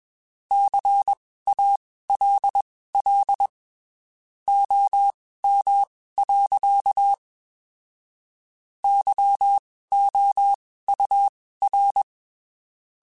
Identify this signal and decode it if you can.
Morse code, heard as a keyed tone, about 21 wpm, spaced as for 15 wpm, 800 Hz. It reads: CALL OM. YOUR
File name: (spaced) / QSOs Spaced-mp3s / kc7sac @20 sp